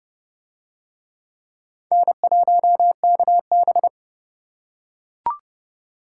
Morse code, heard as a keyed tone, about 28 wpm, 700 Hz. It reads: N1KB